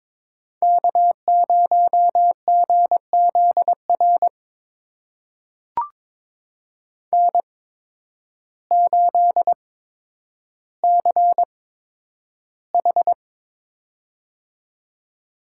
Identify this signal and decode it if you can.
Morse code, heard as a keyed tone, about 22 wpm, spaced as for 10 wpm, 700 Hz. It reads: K0GZR N8CH
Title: USA Call Signs - 8x Character Spacing - 22wpm